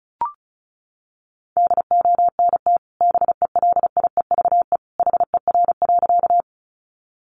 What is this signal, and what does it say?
Morse code, heard as a keyed tone, about 34 wpm, 700 Hz. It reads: DONT BELIEVE HER.
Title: Sentences from Top 1000 Words 35wpm